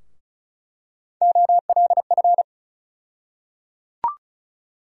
Morse code, heard as a keyed tone, about 34 wpm, 700 Hz. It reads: OLF
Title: USA Call Sign Postfixes - 3 Letters 35wpm